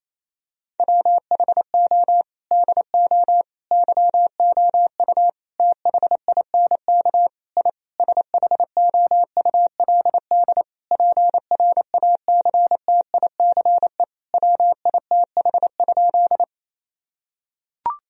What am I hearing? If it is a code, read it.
Morse code, heard as a keyed tone, about 28 wpm, 700 Hz. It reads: WHO DO YOU THINK I SHOULD PRACTICE WITH?
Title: Sentences from Top 1000 Words - Review Part 4 - 28wpm